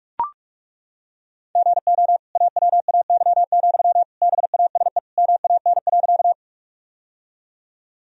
Morse code, heard as a keyed tone, about 42 wpm, 700 Hz. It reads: GO AWAY, BASE MAN.
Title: Sentences from Top 1000 Words - Review Part 1 - 45wpm